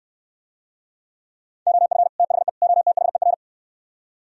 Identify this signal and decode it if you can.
Morse code, heard as a keyed tone, about 67 wpm, 700 Hz. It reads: OF THE OTHER